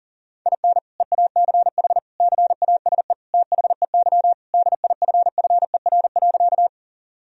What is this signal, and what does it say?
Morse code, heard as a keyed tone, about 37 wpm, 700 Hz. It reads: IN EACH CASE THEY DIFFER.
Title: Sentences from Top 1000 Words - Review Part 1 - 40wpm